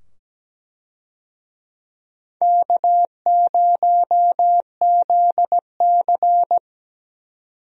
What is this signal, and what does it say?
Morse code, heard as a keyed tone, about 17 wpm, 700 Hz. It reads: K0ZC
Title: USA Call Signs - 4 Characters - 17wpm